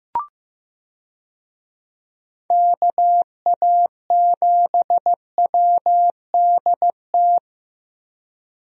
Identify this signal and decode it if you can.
Morse code, heard as a keyed tone, about 15 wpm, 700 Hz. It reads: KA7WDT
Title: USA Call Signs - 6 Characters - Repeated 3x - 15wpm